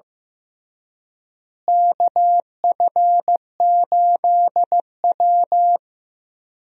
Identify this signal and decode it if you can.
Morse code, heard as a keyed tone, about 15 wpm, 700 Hz. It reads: KF8W
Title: USA Call Signs - 4 Characters - Repeated 3x - 15wpm